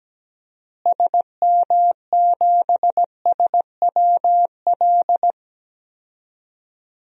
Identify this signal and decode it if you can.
Morse code, heard as a keyed tone, about 17 wpm, 700 Hz. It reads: SM7SWL
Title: DXCC-100 Call Signs - 6 Characters - Repeated 2x - 17wpm